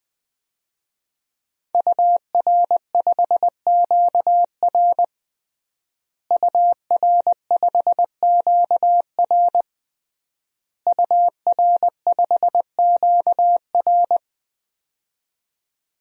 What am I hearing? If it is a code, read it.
Morse code, heard as a keyed tone, about 20 wpm, 700 Hz. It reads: UR5QR UR5QR UR5QR
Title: DXCC-100 Call Signs - 5 Characters - Repeated 3x - 20wpm